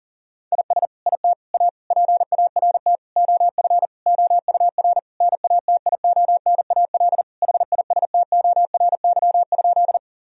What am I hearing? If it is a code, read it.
Morse code, heard as a keyed tone, about 40 wpm, 700 Hz. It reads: IS IT A PART OF OUR NATIONAL HISTORY?
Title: Sentences from Top 1000 Words - Review Part 4 - 40wpm